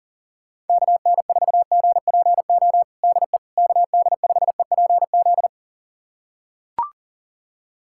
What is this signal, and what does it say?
Morse code, heard as a keyed tone, about 40 wpm, 700 Hz. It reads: KN4GPO DE KD5EPZ
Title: USA Callsign DE Callsign - 6 Characters - 40wpm